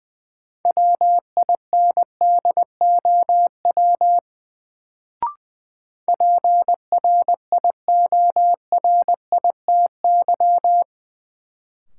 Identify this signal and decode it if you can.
Morse code, heard as a keyed tone, about 20 wpm, 700 Hz. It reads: WINDOW PRIORITY
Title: Sentences from Top 1600 Words - New Word Drill - 20wpm